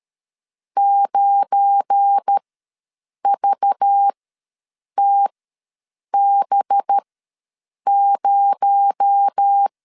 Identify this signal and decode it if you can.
Morse code, heard as a keyed tone, about 13 wpm, spaced as for 7 wpm, 800 Hz. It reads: 9VTB0
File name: LEZIONI / 7 GF9Z / 7 GF9 ws 12wpm 3344